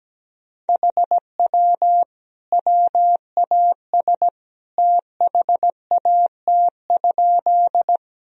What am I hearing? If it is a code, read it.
Morse code, heard as a keyed tone, about 17 wpm, 700 Hz. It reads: HW WAS THAT?
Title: Sentences from Top 100 Words and Abbreviations 17wpm